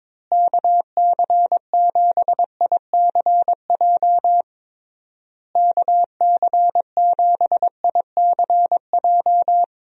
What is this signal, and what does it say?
Morse code, heard as a keyed tone, about 22 wpm, 700 Hz. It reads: KC7ICJ KC7ICJ
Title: USA Call Signs - 6 Characters - Repeated 3x - 22wpm